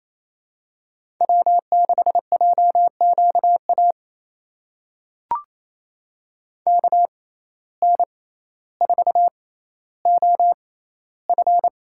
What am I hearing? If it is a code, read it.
Morse code, heard as a keyed tone, about 27 wpm, spaced as for 14 wpm, 700 Hz. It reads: W6JQA KN4OF
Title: USA Call Signs - 6x Character Spacing - 28wpm